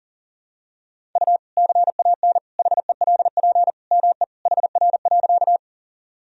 Morse code, heard as a keyed tone, about 40 wpm, 700 Hz. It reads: U CAN HELP ME HR.
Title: Sentences from Top 200 Words and Abbreviations 40wpm